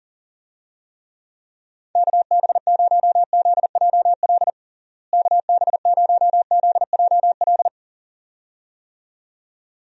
Morse code, heard as a keyed tone, about 40 wpm, 700 Hz. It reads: KB0ZJL KB0ZJL
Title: USA Call Signs - 6 Characters - Repeated 2x - 40wpm